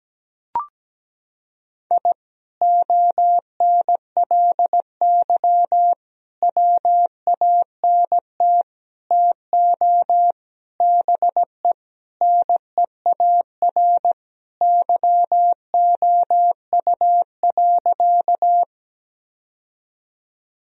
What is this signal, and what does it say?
Morse code, heard as a keyed tone, about 17 wpm, 700 Hz. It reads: I ONLY WANT TO BE NEAR YOU.
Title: Sentences from Top 1000 Words - Review Part 1 - 17wpm